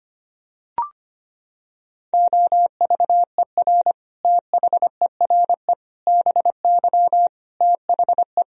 Morse code, heard as a keyed tone, about 25 wpm, 700 Hz. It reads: OVER THERE BY THE
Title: Sentences from Top 1000 Words - Review Part 3 - 25wpm